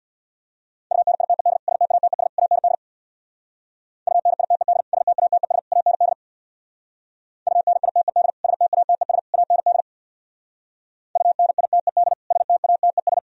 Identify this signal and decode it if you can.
Morse code, heard as a keyed tone, about 64 wpm, 700 Hz. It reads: UNITED STATES AND UNITED STATES AND UNITED STATES AND UNITED STATES